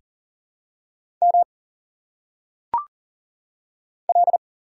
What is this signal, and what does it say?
Morse code, heard as a keyed tone, about 40 wpm, 700 Hz. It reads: M L
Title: Single Letters - Phonetic Pronunciation 40wpm